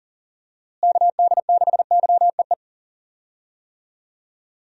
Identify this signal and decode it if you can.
Morse code, heard as a keyed tone, about 40 wpm, 700 Hz. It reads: KD6YEE
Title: USA Call Signs - 6 Characters - Repeated 3x - 40wpm